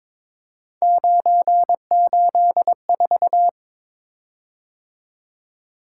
Morse code, heard as a keyed tone, about 22 wpm, 700 Hz. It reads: 984
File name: practice / 22wpm / Three Letter-Number Combinations - 22wpm